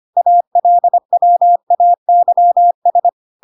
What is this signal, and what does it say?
Morse code, heard as a keyed tone, about 25 wpm, 700 Hz. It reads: ALWAYS
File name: Pinocchio / Pinocchio ch8 25 25 ik1hge